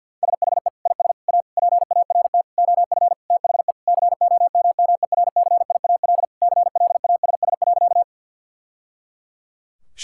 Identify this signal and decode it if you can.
Morse code, heard as a keyed tone, about 52 wpm, 700 Hz. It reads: SHE IS A PART OF THE COMMERCIAL CLASS.
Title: Sentences from Top 1000 Words - Review Part 5 - 50wpm